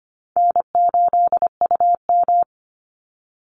Morse code, heard as a keyed tone, about 25 wpm, 700 Hz. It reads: N8UM
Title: Call Signs Repeated 3x - 25wpm